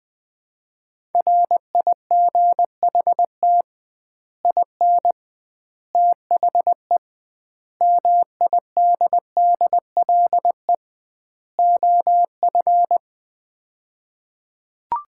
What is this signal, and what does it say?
Morse code, heard as a keyed tone, about 20 wpm, 700 Hz. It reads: RIGHT IN THE MIDDLE OF
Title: 2x Word-Spacing - Sets of 5 Words - 20wpm